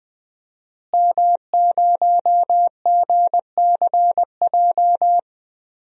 Morse code, heard as a keyed tone, about 20 wpm, 700 Hz. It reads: M0GCJ
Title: DXCC-100 Call Signs - 5 Characters - Repeated 2x - 20wpm